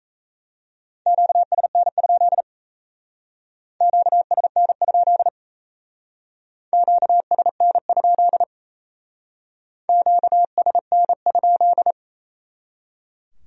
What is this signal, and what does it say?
Morse code, heard as a keyed tone, about 37 wpm, 700 Hz. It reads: QSN? QSN? QSN? QSN?